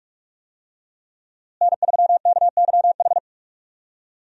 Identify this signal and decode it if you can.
Morse code, heard as a keyed tone, about 41 wpm, 700 Hz. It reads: N3KYH
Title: USA Call Signs - 5 Characters - Repeated 2x - 45wpm